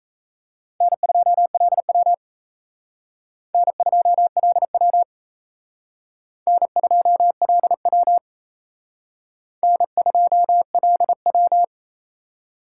Morse code, heard as a keyed tone, about 35 wpm, 700 Hz. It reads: N2LW N2LW N2LW N2LW